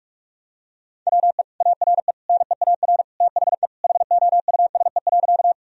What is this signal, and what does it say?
Morse code, heard as a keyed tone, about 42 wpm, 700 Hz. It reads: WE ARE NEAR THE HOUSE.
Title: Sentences from Top 1000 Words - Review Part 1 - 45wpm